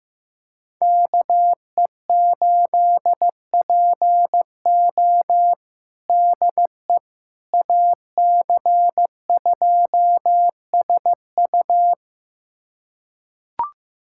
Morse code, heard as a keyed tone, about 15 wpm, 700 Hz. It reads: KE8PO DE AC2SU
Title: USA Callsign DE Callsign - 5 Characters - 15wpm